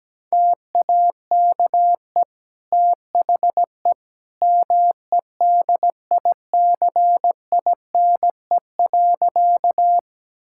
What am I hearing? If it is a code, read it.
Morse code, heard as a keyed tone, about 17 wpm, 700 Hz. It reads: TAKE THE MEDICINE.
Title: Sentences from Top 1000 Words - Review Part 6 - 17wpm